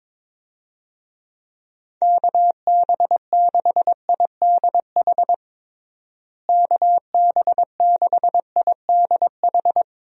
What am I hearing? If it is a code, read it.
Morse code, heard as a keyed tone, about 22 wpm, 700 Hz. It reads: KB6IDH KB6IDH
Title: USA Call Signs - 6 Characters - Repeated 2x - 22wpm